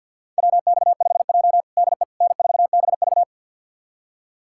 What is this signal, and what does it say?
Morse code, heard as a keyed tone, about 52 wpm, 700 Hz. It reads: WX5J DE N4BV
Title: USA Callsign DE Callsign - 4 Characters - 50wpm